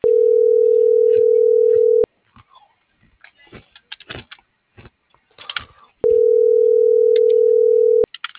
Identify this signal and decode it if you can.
Ringback tone (North America), 440 + 480 Hz.